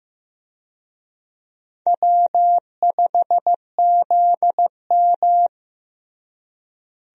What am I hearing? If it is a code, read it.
Morse code, heard as a keyed tone, about 15 wpm, 700 Hz. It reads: W5ZM
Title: USA Call Signs - 4 Characters - Repeated 2x - 15wpm